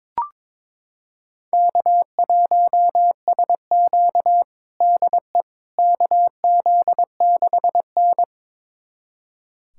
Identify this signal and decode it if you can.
Morse code, heard as a keyed tone, about 23 wpm, 700 Hz. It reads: K1SQ DE KZ6N
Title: USA Callsign DE Callsign - 4 Characters - 22wpm